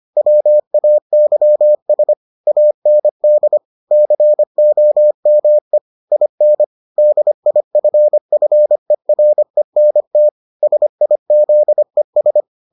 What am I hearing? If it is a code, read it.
Morse code, heard as a keyed tone, about 25 wpm, 600 Hz. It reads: WAYS AND COME IN DIFFERENT SIZES